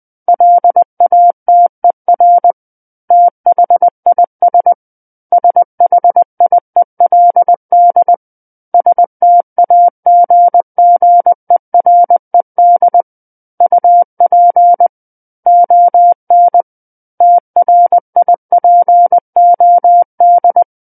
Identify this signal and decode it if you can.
Morse code, heard as a keyed tone, about 20 wpm, 700 Hz. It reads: LATER THIS SHIELD STAGGERED UP ON TRIPOD
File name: 20 WPM